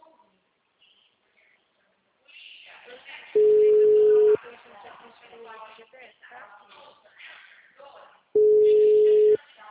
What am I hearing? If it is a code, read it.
Ringback tone (Europe, 425 Hz), with speech.
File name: out- -2038- - - .40086